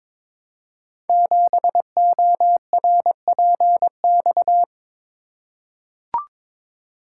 Morse code, heard as a keyed tone, about 22 wpm, 700 Hz. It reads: 7ORPX
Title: Code Groups - 5 Letter:Numbers - 22wpm